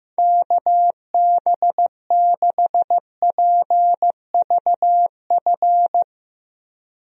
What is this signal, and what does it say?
Morse code, heard as a keyed tone, about 15 wpm, 700 Hz. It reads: KB6PVF
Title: USA Call Signs - 6 Characters - Repeated 2x - 15wpm